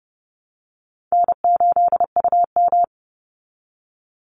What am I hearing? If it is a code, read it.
Morse code, heard as a keyed tone, about 30 wpm, 700 Hz. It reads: N8UM
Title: Call Signs Repeated 3x - 30wpm